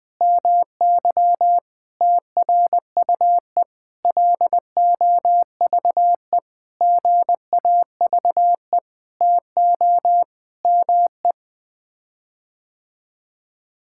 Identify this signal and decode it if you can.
Morse code, heard as a keyed tone, about 20 wpm, 700 Hz. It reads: MY TRUE LOVE GAVE TO ME
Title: Sets of 9 Words - 20wpm